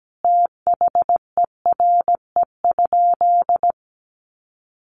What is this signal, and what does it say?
Morse code, heard as a keyed tone, about 17 wpm, 700 Hz. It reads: THERE?
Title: Sentences from Top 800 Words 17wpm